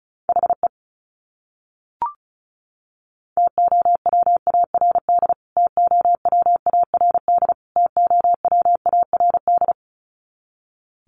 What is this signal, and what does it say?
Morse code, heard as a keyed tone, about 34 wpm, 700 Hz. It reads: HE TOWARD TOWARD TOWARD
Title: Top 500 Words Repeated 35wpm